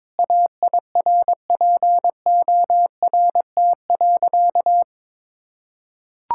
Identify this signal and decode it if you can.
Morse code, heard as a keyed tone, about 22 wpm, 700 Hz. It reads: AIRPORT.
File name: practice / 22wpm / Sentences with 3-Word Phrases - 22wpm